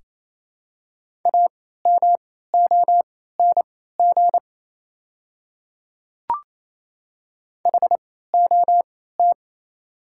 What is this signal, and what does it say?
Morse code, heard as a keyed tone, about 28 wpm, spaced as for 16 wpm, 700 Hz. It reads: AMONG HOT
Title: 3x Character Spacing - Top 500 Words - 28wpm